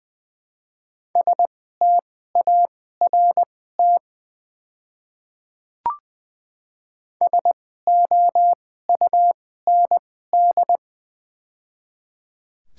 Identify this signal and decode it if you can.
Morse code, heard as a keyed tone, about 21 wpm, spaced as for 15 wpm, 700 Hz. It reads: START SOUND
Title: 2x Character Spacing - Top 500 Words - 20wpm